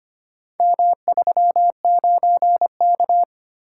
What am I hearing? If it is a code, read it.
Morse code, heard as a keyed tone, about 25 wpm, 700 Hz. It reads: M39K
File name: practice / 25wpm / Four Letter-Number Combinations - 25wpm